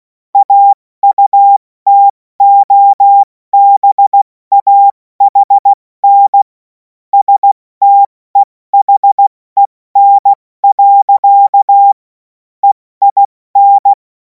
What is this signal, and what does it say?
Morse code, heard as a keyed tone, about 16 wpm, spaced as for 14 wpm, 800 Hz. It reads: AUTOBAHN STEHEN. EIN
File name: Morsekurs / Witze / Witz 05. Mai 2026 70 F80